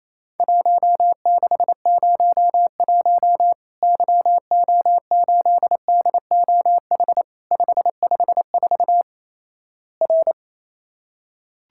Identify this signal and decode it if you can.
Morse code, heard as a keyed tone, about 28 wpm, 700 Hz. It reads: 1601 YO8DOH 554 R
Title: Worked All Europe DX Contest - QTC Exchange - 28wpm